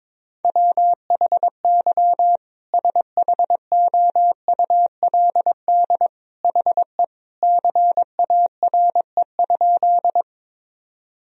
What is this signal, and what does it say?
Morse code, heard as a keyed tone, about 22 wpm, 700 Hz. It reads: WHY SHOULD HE CARE?
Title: Sentences from Top 1000 Words - Review Part 1 - 22wpm